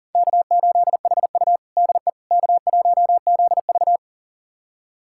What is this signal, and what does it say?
Morse code, heard as a keyed tone, about 40 wpm, 700 Hz. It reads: K8HU DE K1ZV
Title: USA Callsign DE Callsign - 4 Characters - 40wpm